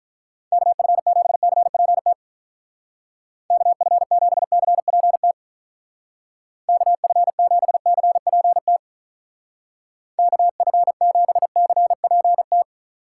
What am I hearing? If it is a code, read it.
Morse code, heard as a keyed tone, about 44 wpm, 700 Hz. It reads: KF7CPT KF7CPT KF7CPT KF7CPT